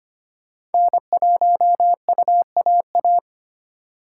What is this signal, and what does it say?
Morse code, heard as a keyed tone, about 33 wpm, 700 Hz. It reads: N1UAA